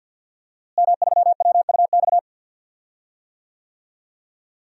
Morse code, heard as a keyed tone, about 50 wpm, 700 Hz. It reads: M3WUX
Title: DXCC-100 Call Signs - 5 Characters - Repeated 3x - 50wpm